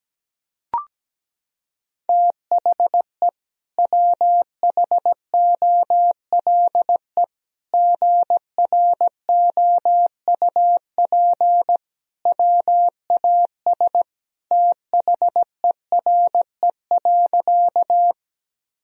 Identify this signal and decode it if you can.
Morse code, heard as a keyed tone, about 17 wpm, 700 Hz. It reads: THE WHOLE GROUP WAS THERE.
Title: Sentences from Top 1000 Words - Review Part 1 - 17wpm